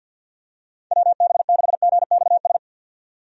Morse code, heard as a keyed tone, about 50 wpm, 700 Hz. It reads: WB6GXS
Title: USA Call Signs - 6 Characters - 50wpm